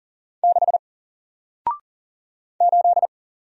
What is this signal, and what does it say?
Morse code, heard as a keyed tone, about 40 wpm, 700 Hz. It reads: 6 8
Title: Single Letter-Number Alt 40wpm